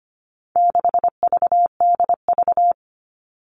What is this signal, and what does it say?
Morse code, heard as a keyed tone, about 25 wpm, 700 Hz. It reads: 6VDV
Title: All Cut Numbers - 4-Digit 25wpm